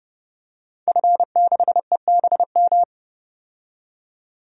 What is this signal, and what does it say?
Morse code, heard as a keyed tone, about 30 wpm, 700 Hz. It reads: F6EBM
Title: DXCC-100 Call Signs - 5 Characters - 30wpm